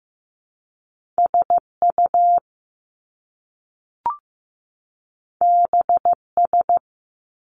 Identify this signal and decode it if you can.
Morse code, heard as a keyed tone, about 15 wpm, 700 Hz. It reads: SU BS
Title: Top Two Letter Combinations in Words 15wpm